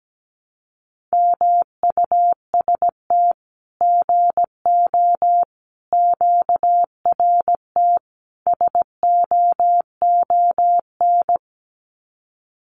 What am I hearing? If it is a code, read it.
Morse code, heard as a keyed tone, about 17 wpm, 700 Hz. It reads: MUST GO QRT SOON
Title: Common QSO Phrases 17wpm